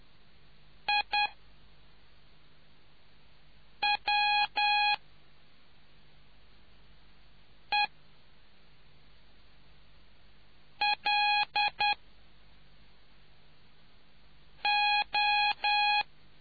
Morse code, heard as a keyed tone, about 10 wpm, spaced as for 3 wpm, 830 Hz. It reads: IWELO